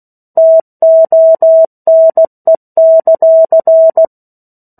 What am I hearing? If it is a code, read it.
Morse code, heard as a keyed tone, about 16 wpm, 650 Hz. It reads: TONE;
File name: 03-Chapter-03 16 WPM